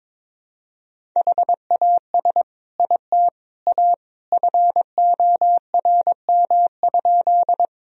Morse code, heard as a keyed tone, about 21 wpm, 700 Hz. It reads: HAS IT A FORM?
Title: Sentences from Top 1000 Words - Review Part 1 - 22wpm